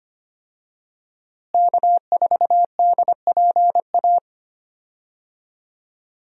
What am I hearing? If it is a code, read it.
Morse code, heard as a keyed tone, about 25 wpm, 700 Hz. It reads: K4DPA
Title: USA Call Signs - 5 Characters - Repeated 2x - 25wpm